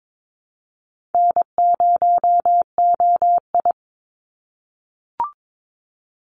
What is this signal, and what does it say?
Morse code, heard as a keyed tone, about 22 wpm, 700 Hz. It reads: N0OI
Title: NA SOTA Chaser Callsigns 22wpm